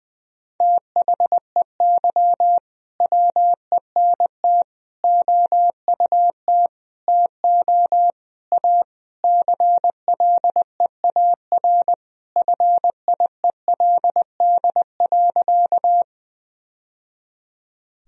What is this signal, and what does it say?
Morse code, heard as a keyed tone, about 19 wpm, 700 Hz. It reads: THEY WENT OUT TO A CLEAR FIELD.
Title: Sentences from Top 1000 Words - Review Part 2 - 20wpm